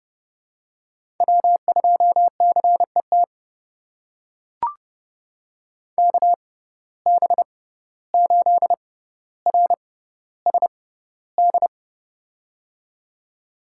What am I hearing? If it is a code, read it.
Morse code, heard as a keyed tone, about 30 wpm, spaced as for 15 wpm, 700 Hz. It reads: W2CET KB8RSD
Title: USA Call Signs - 6x Character Spacing - 30wpm